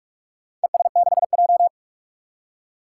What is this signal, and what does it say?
Morse code, heard as a keyed tone, about 42 wpm, 700 Hz. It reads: ES6J
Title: Four Challenging Letter-Number Combinations 45wpm